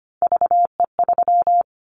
Morse code, heard as a keyed tone, about 25 wpm, 690 Hz. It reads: VE3
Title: Callsign Prefixes for Top 50 DXCC - 25wpm